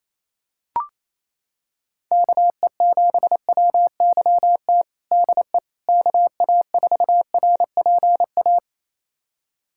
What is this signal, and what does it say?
Morse code, heard as a keyed tone, about 27 wpm, 700 Hz. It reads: KE7WYT DE KA4RPA
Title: USA Callsign DE Callsign 28wpm